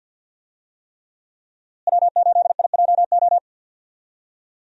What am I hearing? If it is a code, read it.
Morse code, heard as a keyed tone, about 52 wpm, 700 Hz. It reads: W8IJO
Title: USA Call Signs - 5 Characters - Repeated 3x - 50wpm